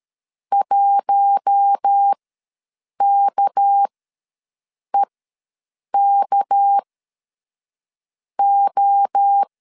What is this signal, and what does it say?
Morse code, heard as a keyed tone, about 13 wpm, spaced as for 6 wpm, 800 Hz. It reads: 1KEKO
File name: LEZIONI / 6 PKX0 / 6 PK ws 12wpm 3116